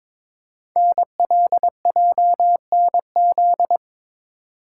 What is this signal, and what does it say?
Morse code, heard as a keyed tone, about 22 wpm, 700 Hz. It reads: NLJNZ
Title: Code Groups - 5 Letters - 22wpm